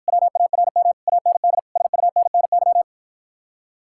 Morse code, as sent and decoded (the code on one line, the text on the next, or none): .-- .- .-. -- / .- -. -.. / ... ..- -. -. -.--
WARM AND SUNNY